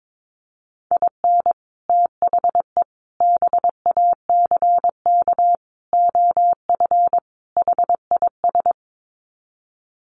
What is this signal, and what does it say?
Morse code, heard as a keyed tone, about 22 wpm, 700 Hz. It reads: IN THE BACK OF HIS
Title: Sets of 5 Words Encore2 22wpm